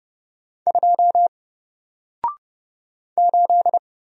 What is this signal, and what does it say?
Morse code, heard as a keyed tone, about 30 wpm, 700 Hz. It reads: 2 8
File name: Single-Letter-Number-30wpm